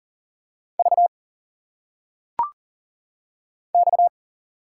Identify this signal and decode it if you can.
Morse code, heard as a keyed tone, about 40 wpm, 700 Hz. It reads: V X
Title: Single Letters - Phonetic Pronunciation 40wpm